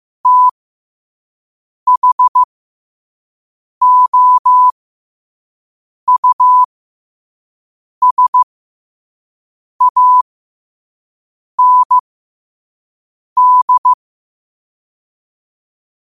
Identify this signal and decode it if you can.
Morse code, heard as a keyed tone, about 15 wpm, spaced as for 5 wpm, 1000 Hz. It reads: THOUSAND
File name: qotdfiles / 5wpm